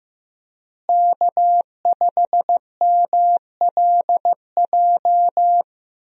Morse code, heard as a keyed tone, about 15 wpm, 700 Hz. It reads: K5MLJ
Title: USA Call Signs - 5 Characters - Repeated 2x - 15wpm